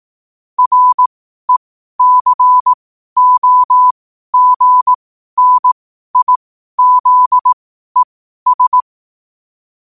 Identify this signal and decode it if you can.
Morse code, heard as a keyed tone, about 18 wpm, spaced as for 12 wpm, 1000 Hz. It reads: RECOGNIZES
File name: qotdfiles / 12wpm